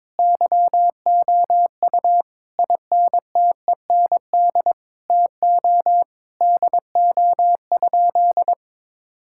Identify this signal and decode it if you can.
Morse code, heard as a keyed tone, about 22 wpm, 700 Hz. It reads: YOU INTEND TO DO?
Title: Sentences from Top 1600 Words 22wpm